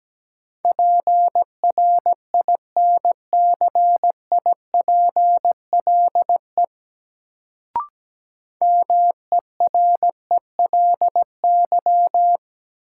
Morse code, heard as a keyed tone, about 17 wpm, 700 Hz. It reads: PRINCIPLE MERELY
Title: Sentences from Top 1400 Words - New Word Drill - 17wpm